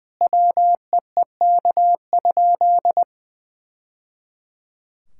Morse code, heard as a keyed tone, about 20 wpm, 700 Hz. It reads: WEEK?
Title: Sentences from Top 1000 Words - Review Part 5 - 20wpm